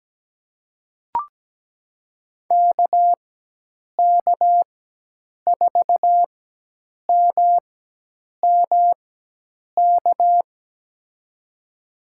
Morse code, heard as a keyed tone, about 17 wpm, spaced as for 8 wpm, 700 Hz. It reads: KK4MMK
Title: USA Call Signs - 4x Character Spacing - 17wpm